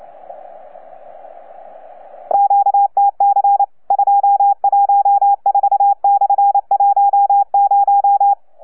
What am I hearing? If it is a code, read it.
Morse code, heard as a keyed tone, about 28 wpm, 810 Hz. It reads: QTC 214/10
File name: UP2L-20-CW